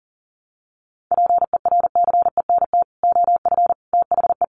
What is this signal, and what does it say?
Morse code, heard as a keyed tone, about 40 wpm, 700 Hz. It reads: PERCENT OF THE
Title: Sets of 3 Words 40wpm